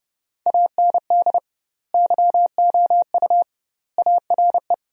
Morse code, heard as a keyed tone, about 30 wpm, 700 Hz. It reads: AND YOU ARE
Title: 2x Word-Spacing - Sentences from Top 400 Words - 30wpm